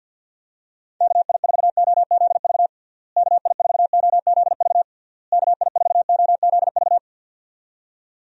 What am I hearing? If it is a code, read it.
Morse code, heard as a keyed tone, about 51 wpm, 700 Hz. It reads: KI4OZV KI4OZV KI4OZV
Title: USA Call Signs - 6 Characters - Repeated 3x - 50wpm